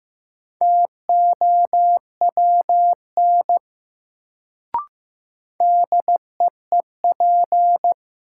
Morse code, heard as a keyed tone, about 15 wpm, 700 Hz. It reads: TOWN DEEP
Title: Sentences from Top 500 Words - New Word Drill - 15wpm